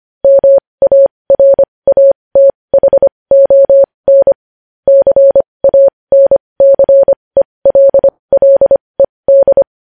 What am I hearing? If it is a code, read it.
Morse code, heard as a keyed tone, about 25 wpm, spaced as for 20 wpm, 550 Hz. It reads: MARATHON CANCELLED
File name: podcast / News headlines-1x-25x20